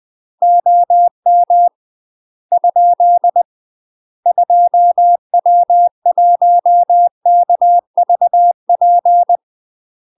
Morse code, heard as a keyed tone, about 20 wpm, 700 Hz. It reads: OM ? 2W1KVP
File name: UK MASTER QSOs TRAINING / 20WPM / UK MASTER QSOs 0016